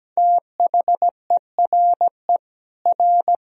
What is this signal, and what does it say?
Morse code, heard as a keyed tone, about 17 wpm, 700 Hz. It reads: THERE R
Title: Sentences from Top 400 Words and Abbreviations 17wpm